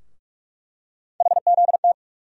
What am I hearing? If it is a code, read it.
Morse code, heard as a keyed tone, about 41 wpm, 700 Hz. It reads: HZT
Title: USA Call Sign Postfixes - 3 Letters 45wpm